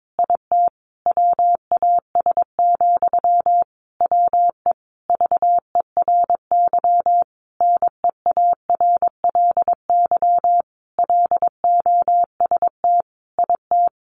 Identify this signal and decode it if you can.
Morse code, heard as a keyed tone, about 22 wpm, 700 Hz. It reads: IT WAS, WE VERY NEARLY LOST IT
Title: Sentences from Top 1400 Words 22wpm